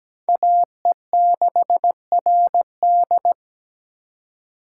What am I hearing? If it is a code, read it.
Morse code, heard as a keyed tone, about 17 wpm, 700 Hz. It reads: AE6RD
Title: USA Call Signs - 5 Characters - 2x Character Spacing - 17wpm